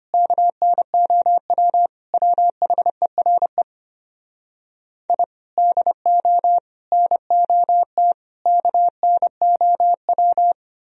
Morse code, heard as a keyed tone, about 33 wpm, 700 Hz. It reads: KNOW WHERE I DO NOT KNOW